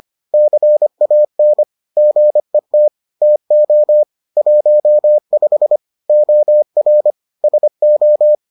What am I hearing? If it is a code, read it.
Morse code, heard as a keyed tone, about 25 wpm, 600 Hz. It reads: CAN GET TO 15 OR SO